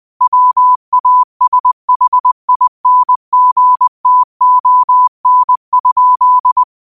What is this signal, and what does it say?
Morse code, heard as a keyed tone, about 20 wpm, 1000 Hz. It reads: WASHINGTON?